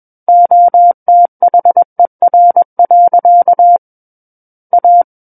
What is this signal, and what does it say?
Morse code, heard as a keyed tone, about 21 wpm, 700 Hz. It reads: OTHER. A
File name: Book One / 12-Chapter 12 - 21 WPM